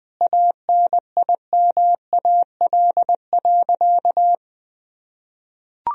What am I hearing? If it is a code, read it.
Morse code, heard as a keyed tone, about 20 wpm, 700 Hz. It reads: ANIMAL.
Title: Sentences from Top 1000 Words - Review Part 1 - 20wpm